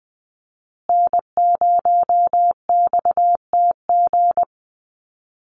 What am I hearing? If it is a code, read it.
Morse code, heard as a keyed tone, about 20 wpm, 700 Hz. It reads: N0XTG
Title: USA Call Signs Repeated 3x 20wpm